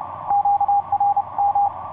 Morse code, heard as a keyed tone, about 28 wpm, 810 Hz. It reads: QRM
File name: py0ff ea7jb qrt 12m cw 14feb2010 1548